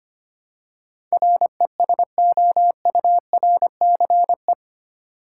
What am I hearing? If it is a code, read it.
Morse code, heard as a keyed tone, about 25 wpm, 700 Hz. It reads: RESOURCE
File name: 8 Letter Words - First 3 Letters Given - 25wpm